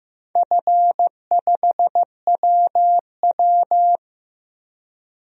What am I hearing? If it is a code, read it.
Morse code, heard as a keyed tone, about 15 wpm, 700 Hz. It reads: F5WW
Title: DXCC-100 Call Signs - 4 Characters - 15wpm